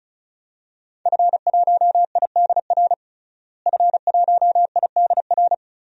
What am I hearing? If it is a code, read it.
Morse code, heard as a keyed tone, about 34 wpm, 700 Hz. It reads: F1IDR F1IDR
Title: DXCC-100 Call Signs - 5 Characters - Repeated 3x - 35wpm